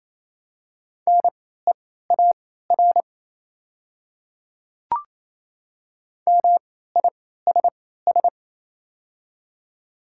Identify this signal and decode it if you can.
Morse code, heard as a keyed tone, about 28 wpm, spaced as for 16 wpm, 700 Hz. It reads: NEAR MISS
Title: 3x Character Spacing - Top 500 Words - 28wpm